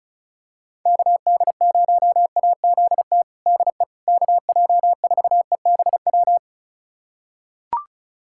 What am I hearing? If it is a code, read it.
Morse code, heard as a keyed tone, about 36 wpm, 700 Hz. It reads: KD0AZT DE KJ4EBW
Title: USA Callsign DE Callsign 35wpm